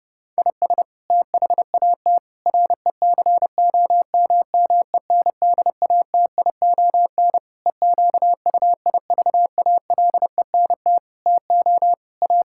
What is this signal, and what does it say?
Morse code, heard as a keyed tone, about 30 wpm, 700 Hz. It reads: IS THAT RECOMMENDATION EQUIVALENT TO A
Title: Sentences from Top 1600 Words 30wpm